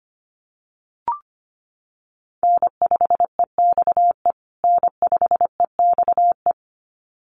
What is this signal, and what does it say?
Morse code, heard as a keyed tone, about 25 wpm, 700 Hz. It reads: N5EXE N5EXE
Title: USA Call Signs Repeated 2x 25wpm